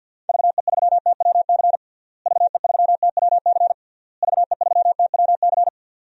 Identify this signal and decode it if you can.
Morse code, heard as a keyed tone, about 52 wpm, 700 Hz. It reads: VE3TWC VE3TWC VE3TWC
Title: DXCC-100 Call Signs - 6 Characters - Repeated 3x - 50wpm